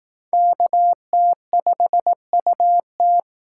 Morse code, heard as a keyed tone, about 18 wpm, 700 Hz. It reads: KT5UT
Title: Speed Racing - USA 2x2 Call Signs 15wpm